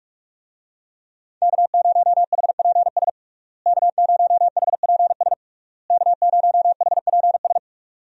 Morse code, heard as a keyed tone, about 46 wpm, 700 Hz. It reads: K0HPS K0HPS K0HPS
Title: USA Call Signs - 5 Characters - Repeated 3x - 45wpm